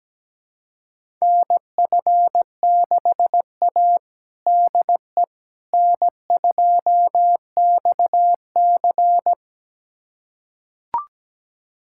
Morse code, heard as a keyed tone, about 17 wpm, 700 Hz. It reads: NF6A DE N2XC
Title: USA Callsign DE Callsign - 4 Characters - 17wpm